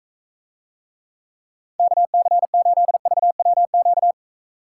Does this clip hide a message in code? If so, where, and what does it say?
Yes: Morse code from 1.8 to 4.1 s, heard as a keyed tone, about 43 wpm, 700 Hz. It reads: KC8VWQ